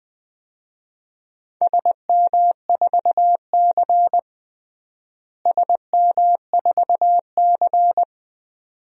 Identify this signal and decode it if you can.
Morse code, heard as a keyed tone, about 21 wpm, 700 Hz. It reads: SM4C SM4C
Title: DXCC-100 Call Signs - 4 Characters - Repeated 2x - 20wpm